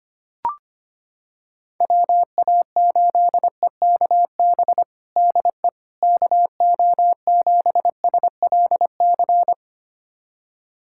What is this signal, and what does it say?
Morse code, heard as a keyed tone, about 25 wpm, 700 Hz. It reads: WA8EKB DE KO7SLC
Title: USA Callsign DE Callsign - 6 Characters - 25wpm